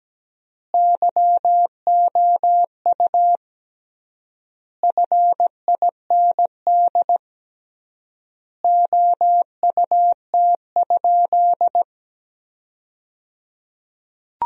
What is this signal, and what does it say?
Morse code, heard as a keyed tone, about 17 wpm, 700 Hz. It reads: YOU FIND OUT?
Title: 3x Word-Spacing - Sentences from Top 100 Words - 17wpm